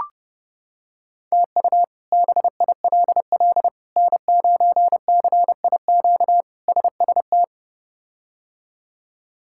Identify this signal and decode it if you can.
Morse code, heard as a keyed tone, about 32 wpm, 700 Hz. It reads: TU BILL N9CIQ SST
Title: K1USN SST QSO Practice - 30wpm